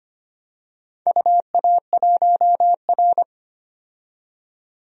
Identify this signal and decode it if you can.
Morse code, heard as a keyed tone, about 25 wpm, 700 Hz. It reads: UA1R
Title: DXCC-100 Call Signs - 4 Characters - 25wpm